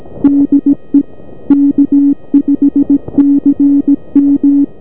Morse code, heard as a keyed tone, about 17 wpm, 280 Hz. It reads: DE K5CM
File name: 40m Snipofk5cmbywa1abi